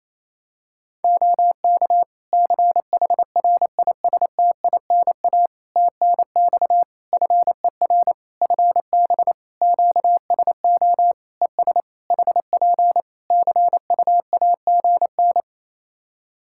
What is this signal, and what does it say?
Morse code, heard as a keyed tone, about 28 wpm, 700 Hz. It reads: OK CHRISTINA TNX FER FB QSO ES HP CUAGN
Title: Practice QSO - Exchange 4 - Ending - 28wpm